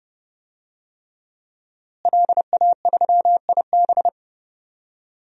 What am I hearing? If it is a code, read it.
Morse code, heard as a keyed tone, about 31 wpm, 700 Hz. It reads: LA3IB
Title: DXCC-100 Call Signs - 5 Characters - Repeated 3x - 30wpm